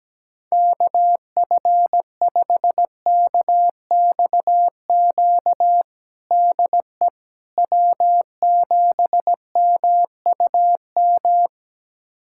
Morse code, heard as a keyed tone, about 17 wpm, 700 Hz. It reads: KF5KXQ DE W7MUM
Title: USA Callsign DE Callsign 17wpm